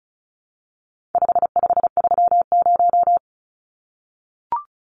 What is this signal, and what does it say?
Morse code, heard as a keyed tone, about 34 wpm, 700 Hz. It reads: 5530
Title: Numbers - 4-Digit 35wpm